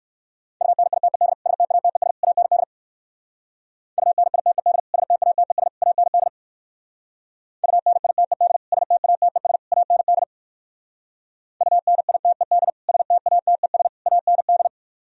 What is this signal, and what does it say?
Morse code, heard as a keyed tone, about 57 wpm, 700 Hz. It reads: UNITED STATES AND UNITED STATES AND UNITED STATES AND UNITED STATES AND